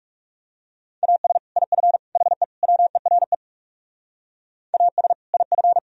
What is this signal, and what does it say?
Morse code, heard as a keyed tone, about 39 wpm, 700 Hz. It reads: AS IF HE WERE AS IF